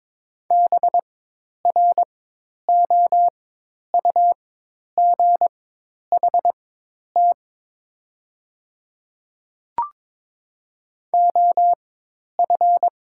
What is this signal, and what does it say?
Morse code, heard as a keyed tone, about 23 wpm, spaced as for 10 wpm, 700 Hz. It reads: BROUGHT OF
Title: 4x Character Spacing - Top 500 Words - 22wpm